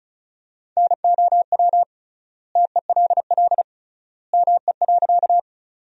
Morse code, heard as a keyed tone, about 34 wpm, 700 Hz. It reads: NOW TELL ME.
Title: 3x Word-Spacing - Sentences from Top 200 Words - 35wpm